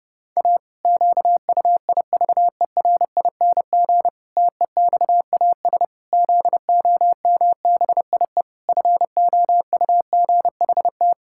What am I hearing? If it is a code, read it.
Morse code, heard as a keyed tone, about 32 wpm, 700 Hz. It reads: A QUIVERING TEXAS ZOMBIE FOUGHT
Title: Pangram Challenge v2 - 30wpm